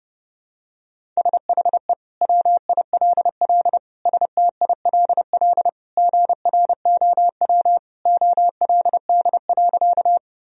Morse code, heard as a keyed tone, about 30 wpm, 700 Hz. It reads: SHE WILL STILL GROW OLD.
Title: Sentences from Top 1000 Words - Review Part 1 - 30wpm